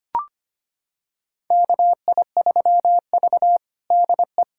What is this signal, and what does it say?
Morse code, heard as a keyed tone, about 25 wpm, 700 Hz. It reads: KI3V DE
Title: USA Callsign DE Callsign 25wpm